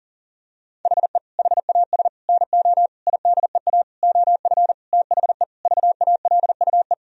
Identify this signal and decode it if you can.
Morse code, heard as a keyed tone, about 40 wpm, 700 Hz. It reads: HE HAS NO IDEA OF THE VALUE
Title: Sentences from Top 1000 Words - Review Part 3 - 40wpm